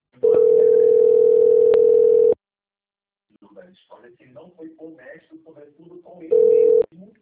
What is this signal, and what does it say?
Ringback tone (North America), 440 + 480 Hz, with speech.
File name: out- - - - - .17014